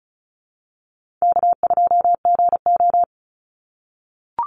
Morse code, heard as a keyed tone, about 34 wpm, 700 Hz. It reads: K2GO
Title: Call Signs - 35wpm